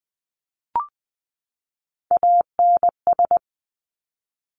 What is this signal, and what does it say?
Morse code, heard as a keyed tone, about 20 wpm, 700 Hz. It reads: ANS
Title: Top Three Letter Combinations in Words 20wpm